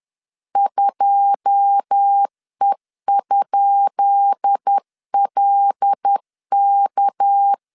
Morse code, heard as a keyed tone, about 10 wpm, 800 Hz. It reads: 2E?LK
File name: LEZIONI / 10 SIGNS / 10D COMA 10wpm 3013